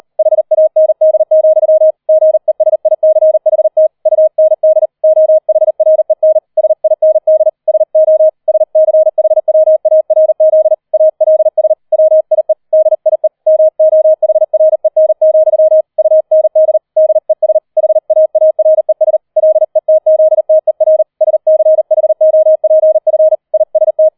Morse code, heard as a keyed tone, about 38 wpm, 630 Hz. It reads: HAND, GESICHT UND OHREN SIND SO SCHWARZ ALS WIE DIE MOHREN, UND DES HAARES LETZTER SCHOPF IST
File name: 004 Vierter Streich 40wpm